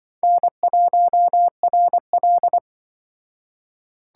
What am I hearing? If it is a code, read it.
Morse code, heard as a keyed tone, about 24 wpm, 700 Hz. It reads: N1RL
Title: Speed Racing - USA 1x2 Call Signs 20wpm